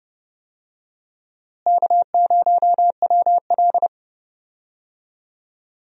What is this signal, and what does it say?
Morse code, heard as a keyed tone, about 31 wpm, 700 Hz. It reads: K0WL
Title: USA Call Signs - 4 Characters - Repeated 3x - 30wpm